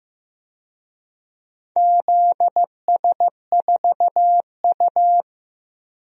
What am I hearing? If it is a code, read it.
Morse code, heard as a keyed tone, about 16 wpm, 700 Hz. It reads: ZS4U
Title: DXCC-100 Call Signs - 4 Characters - 15wpm